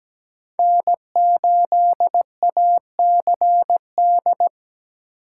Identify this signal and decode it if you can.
Morse code, heard as a keyed tone, about 17 wpm, 700 Hz. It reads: N8ACD
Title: USA Call Signs - 3x Character Spacing - 17wpm